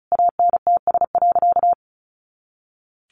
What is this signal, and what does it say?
Morse code, heard as a keyed tone, about 34 wpm, 700 Hz. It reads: ANTS.
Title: Additional Easy Phrases 35wpm